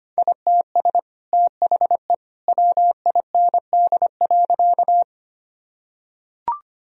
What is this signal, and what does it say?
Morse code, heard as a keyed tone, about 25 wpm, 700 Hz. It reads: ITS THE WIND.
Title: Sentences from Top 1000 Words - Review Part 2 - 25wpm